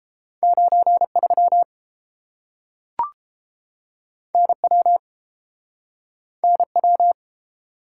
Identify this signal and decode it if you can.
Morse code, heard as a keyed tone, about 30 wpm, 700 Hz. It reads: 93 NW NW